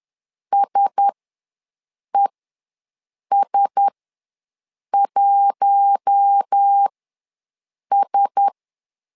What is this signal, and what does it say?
Morse code, heard as a keyed tone, about 10 wpm, spaced as for 6 wpm, 800 Hz. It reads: SES1S
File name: LEZIONI / 4 Q1S ws 10wpm 2208